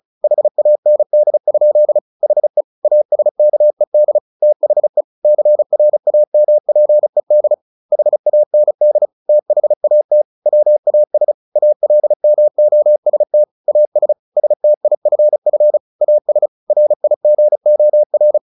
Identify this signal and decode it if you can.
Morse code, heard as a keyed tone, about 35 wpm, 600 Hz. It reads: HAND? HE ASKED THE CRAMPED HAND THAT WAS ALMOST AS STIFF AS RIGOR
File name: the old man and the sea / the old man and the sea 1 35wpm